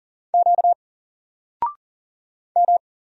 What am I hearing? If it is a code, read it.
Morse code, heard as a keyed tone, about 40 wpm, 700 Hz. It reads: Q M
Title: Single Letter-Number 40wpm